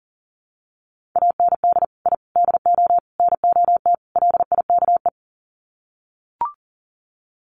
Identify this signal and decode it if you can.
Morse code, heard as a keyed tone, about 40 wpm, 700 Hz. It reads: AND I DO NOT LIKE
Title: Sets of 5 Words Encore2 40wpm